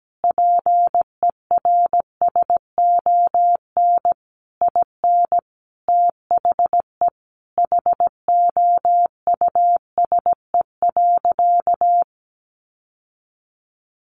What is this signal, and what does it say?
Morse code, heard as a keyed tone, about 17 wpm, 700 Hz. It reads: PERSON IN THE HOUSE.
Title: Sentences from Top 1300 Words 17wpm